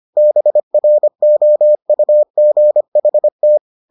Morse code, heard as a keyed tone, about 25 wpm, 600 Hz. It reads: BROUGHT